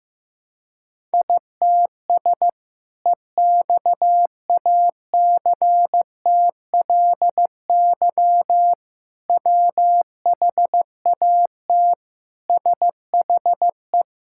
Morse code, heard as a keyed tone, about 15 wpm, 700 Hz. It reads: ITS EXACTLY WHAT SHE
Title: Sentences from Top 1000 Words - Review Part 6 - 15wpm